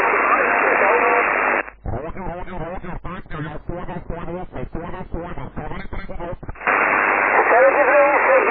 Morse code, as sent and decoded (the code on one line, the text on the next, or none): none